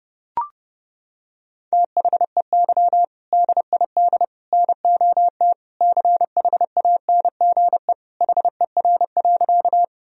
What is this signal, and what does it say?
Morse code, heard as a keyed tone, about 30 wpm, 700 Hz. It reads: THEY DID NOT CHANGE HER.
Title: Sentences from Top 1000 Words - Review Part 1 - 30wpm